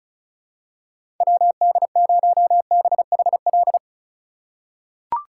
Morse code, heard as a keyed tone, about 36 wpm, 700 Hz. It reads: WD0BHL
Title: USA Call Signs - 6 Characters - 4x Character Spacing - 35wpm